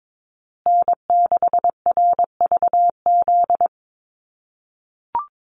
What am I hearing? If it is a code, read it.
Morse code, heard as a keyed tone, about 22 wpm, 700 Hz. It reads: N6RVZ
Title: USA Call Signs Repeated 3x 22wpm